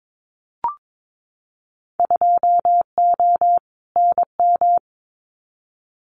Morse code, heard as a keyed tone, about 22 wpm, 700 Hz. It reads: 2O NM
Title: Winter Field Day Exchanges 22wpm